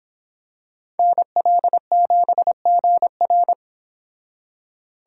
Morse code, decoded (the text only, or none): NL7GR